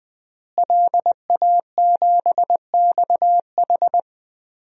Morse code, heard as a keyed tone, about 20 wpm, 700 Hz. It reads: LA7XH
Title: DXCC-100 Call Signs - 5 Characters - Repeated 3x - 20wpm